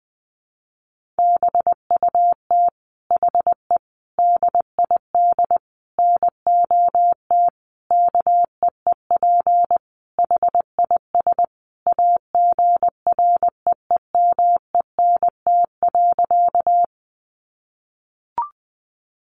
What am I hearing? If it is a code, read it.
Morse code, heard as a keyed tone, about 20 wpm, 700 Hz. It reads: BUT HE DID NOT KEEP HIS AGREEMENT.
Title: Sentences from Top 1500 Words 20wpm